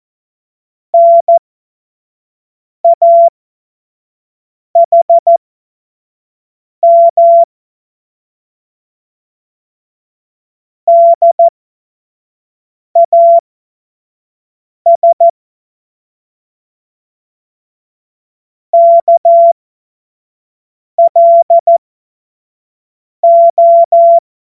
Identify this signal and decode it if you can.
Morse code, heard as a keyed tone, about 14 wpm, spaced as for 5 wpm, 680 Hz. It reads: NAHM DAS KLO